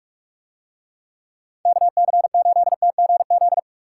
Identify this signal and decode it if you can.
Morse code, heard as a keyed tone, about 44 wpm, 700 Hz. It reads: KC8TGZ